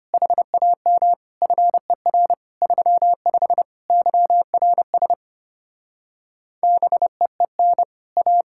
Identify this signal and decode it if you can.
Morse code, heard as a keyed tone, about 33 wpm, 700 Hz. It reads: HAM FER 35 YRS BEEN A